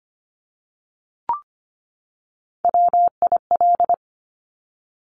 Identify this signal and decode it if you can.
Morse code, heard as a keyed tone, about 25 wpm, 700 Hz. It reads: WIL
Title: Top Three Letter Combinations in Words 25wpm